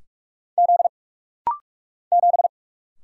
Morse code, heard as a keyed tone, about 44 wpm, 700 Hz. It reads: Z 7
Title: Single Letter-Number - Rapid-Fire 45wpm